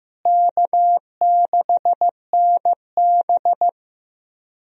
Morse code, heard as a keyed tone, about 15 wpm, 700 Hz. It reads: K6NB
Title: USA Call Signs - 4 Characters - Repeated 2x - 15wpm